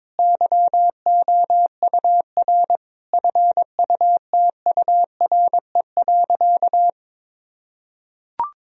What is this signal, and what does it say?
Morse code, heard as a keyed tone, about 22 wpm, 700 Hz. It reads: YOUR FUTURE.
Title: Sentences from Top 1000 Words - Review Part 4 - 22wpm